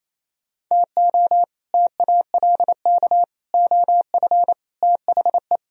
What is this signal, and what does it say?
Morse code, heard as a keyed tone, about 32 wpm, 700 Hz. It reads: TO TALK OF THE
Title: Speed-Racing - Sets of 5 Words 28wpm